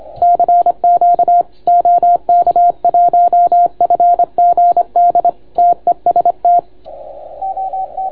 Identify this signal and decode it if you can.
Morse code, heard as a keyed tone, about 26 wpm, 690 Hz, with radio noise. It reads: CQ OK1FGD TEST